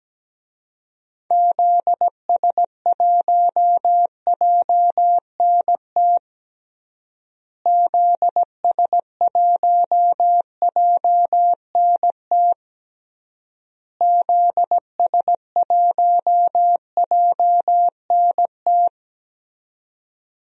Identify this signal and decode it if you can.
Morse code, heard as a keyed tone, about 17 wpm, 700 Hz. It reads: ZS1JNT ZS1JNT ZS1JNT
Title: DXCC-100 Call Signs - 6 Characters - Repeated 3x - 17wpm